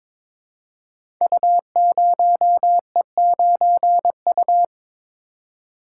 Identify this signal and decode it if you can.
Morse code, heard as a keyed tone, about 22 wpm, 700 Hz. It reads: U0E9U
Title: Code Groups - 5 Letter:Numbers - 22wpm